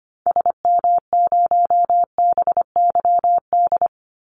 Morse code, heard as a keyed tone, about 25 wpm, 700 Hz. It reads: SM0BYD
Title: Call Signs - 25wpm